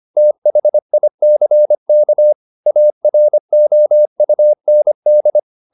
Morse code, heard as a keyed tone, about 25 wpm, 600 Hz. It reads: THICK AROUND
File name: the old man and the sea part 1 25wpm